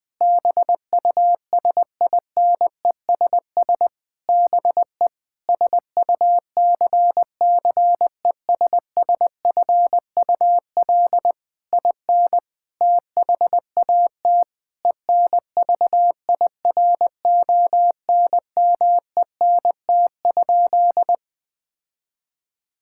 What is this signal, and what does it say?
Morse code, heard as a keyed tone, about 20 wpm, 700 Hz. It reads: BUSINESS BE SUCCESSFUL IN THAT ENVIRONMENT?
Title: Sentences from Top 1000 Words - Review Part 5 - 20wpm